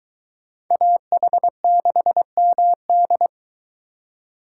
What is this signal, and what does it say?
Morse code, heard as a keyed tone, about 22 wpm, 700 Hz. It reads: AH6MD